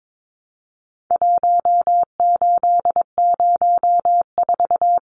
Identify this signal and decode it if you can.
Morse code, heard as a keyed tone, about 22 wpm, 700 Hz. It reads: 1804
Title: Numbers - Contest Style - 4-Digit 22wpm